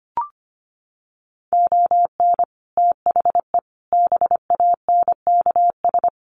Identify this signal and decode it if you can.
Morse code, heard as a keyed tone, about 25 wpm, 700 Hz. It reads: ON THE BANKS
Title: Sets of 5 Words Encore2 25wpm